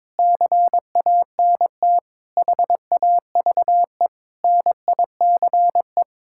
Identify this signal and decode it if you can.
Morse code, heard as a keyed tone, about 22 wpm, 700 Hz. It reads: CANT HAVE NICE
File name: practice / 22wpm / Sets of 8 Words - 22wpm